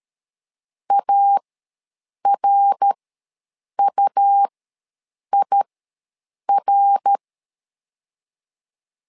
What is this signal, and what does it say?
Morse code, heard as a keyed tone, about 13 wpm, spaced as for 7 wpm, 800 Hz. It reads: ARUIR
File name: LEZIONI / 3 MWRO / 3 MWR ws 12wpm 1809